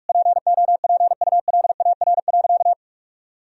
Morse code, heard as a keyed tone, about 46 wpm, 700 Hz. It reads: POPULAR.
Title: Sentences from Top 1000 Words - Review Part 3 - 45wpm